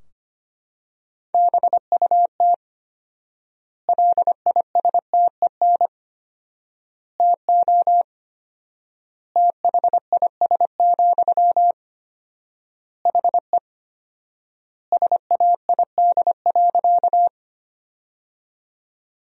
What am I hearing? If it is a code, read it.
Morse code, heard as a keyed tone, about 25 wpm, 700 Hz. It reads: BUT LISTEN TO THIS, HE SAID.
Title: 4x Word-Spacing - Sentences from Top 400 Words - 25wpm